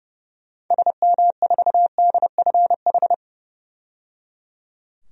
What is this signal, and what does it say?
Morse code, heard as a keyed tone, about 30 wpm, 700 Hz. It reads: SM4DFH
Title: DXCC-100 Call Signs - 6 Characters - 30wpm